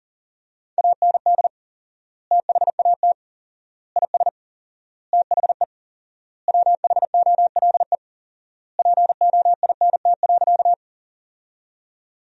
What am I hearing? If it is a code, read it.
Morse code, heard as a keyed tone, about 40 wpm, 700 Hz. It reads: AND THAT IS THE WHOLE POINT.
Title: 4x Word-Spacing - Sentences from Top 400 Words - 40wpm